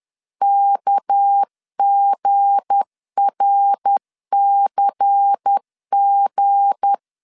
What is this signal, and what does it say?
Morse code, heard as a keyed tone, about 10 wpm, 800 Hz. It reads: KGRCG
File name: CW / DET / 1 9 1500C S B 10wpm 4439